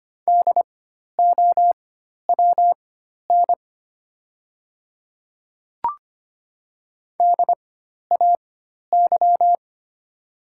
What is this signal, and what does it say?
Morse code, heard as a keyed tone, about 25 wpm, spaced as for 12 wpm, 700 Hz. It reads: DOWN DAY
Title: 4x Character Spacing - Top 100 Words - 25wpm